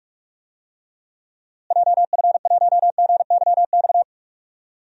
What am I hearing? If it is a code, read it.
Morse code, heard as a keyed tone, about 43 wpm, 700 Hz. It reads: JF1GYX
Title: DXCC-100 Call Signs - 6 Characters - Repeated 3x - 45wpm